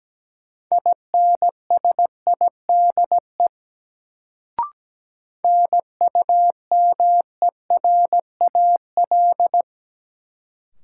Morse code, heard as a keyed tone, about 17 wpm, 700 Hz. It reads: INSIDE NUMERAL
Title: Sentences from Top 1000 Words - Review Part 2 - New Word Drill - 17wpm